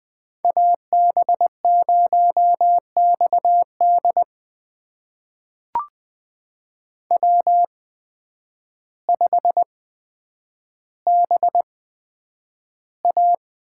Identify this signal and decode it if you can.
Morse code, heard as a keyed tone, about 20 wpm, spaced as for 9 wpm, 700 Hz. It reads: AB0XD W5BA
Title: USA Call Signs - 5 Characters - 8x Character Spacing - 20wpm